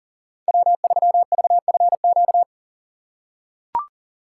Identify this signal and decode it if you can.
Morse code, heard as a keyed tone, about 40 wpm, 700 Hz. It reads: W3VFQ
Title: USA Call Signs - 5 Characters - 4x Character Spacing - 40wpm